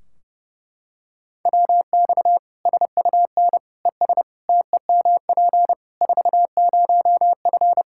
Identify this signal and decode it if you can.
Morse code, heard as a keyed tone, about 32 wpm, 700 Hz. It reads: WX SUN ES TEMP 40F
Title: Practice QSO - Exchange 2 - Info - 30wpm